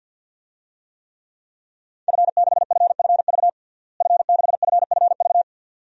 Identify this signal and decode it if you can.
Morse code, heard as a keyed tone, about 51 wpm, 700 Hz. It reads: F6FFV F6FFV
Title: DXCC-100 Call Signs - 5 Characters - Repeated 2x - 50wpm